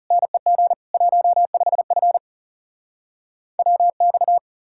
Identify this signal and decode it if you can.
Morse code, heard as a keyed tone, about 43 wpm, 700 Hz. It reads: NEG 15F WX